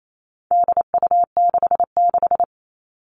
Morse code, heard as a keyed tone, about 28 wpm, 700 Hz. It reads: DU66
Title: All Cut Numbers - 4-Digit 28wpm